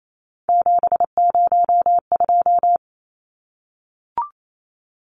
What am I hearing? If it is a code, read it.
Morse code, heard as a keyed tone, about 28 wpm, 700 Hz. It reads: 702
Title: Numbers - 3-Digit 28wpm